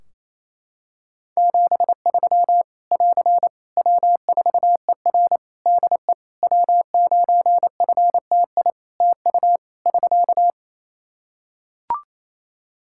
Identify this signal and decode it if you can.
Morse code, heard as a keyed tone, about 28 wpm, 700 Hz. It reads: 73 <AR> W4ER DE W9FTI TU <SK>
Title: Practice QSO - Exchange 4 - Ending - 28wpm